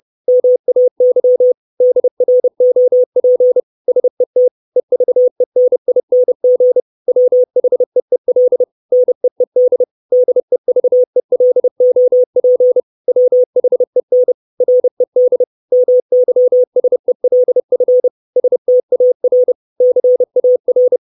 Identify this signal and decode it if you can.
Morse code, heard as a keyed tone, about 30 wpm, 500 Hz. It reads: MAY DROP SET EVENING WHEEL NEED DEVELOP WHEN RED MYSELF STAR CAR
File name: 632 Words 30WPM